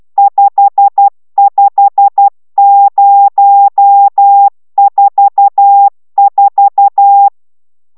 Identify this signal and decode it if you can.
Morse code, heard as a keyed tone, about 12 wpm, 800 Hz. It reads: 55044
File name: taviro tanulas / cw 69 20 10